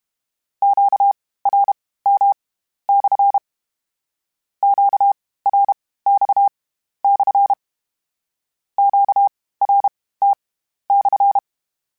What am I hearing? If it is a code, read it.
Morse code, heard as a keyed tone, about 32 wpm, spaced as for 16 wpm, 800 Hz. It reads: QRM/ QRX/ QRT/
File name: Q Calls / Q Calls-mp3s / Q Calls SPSL1 @30